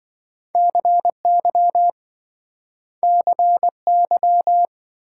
Morse code, heard as a keyed tone, about 27 wpm, 700 Hz. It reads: CY CY